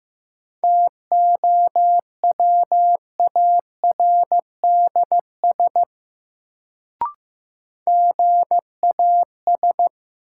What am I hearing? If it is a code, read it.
Morse code, heard as a keyed tone, about 15 wpm, 700 Hz. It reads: TOWARDS GAS
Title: Sentences from Top 1100 Words - New Word Drill Alt - 15wpm